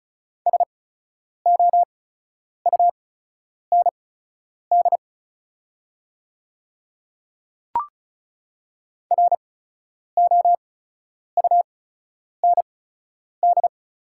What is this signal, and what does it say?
Morse code, heard as a keyed tone, about 34 wpm, spaced as for 10 wpm, 700 Hz. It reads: SOUND ROUND
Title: 8x Character Spacing - Top 200 Words - 35wpm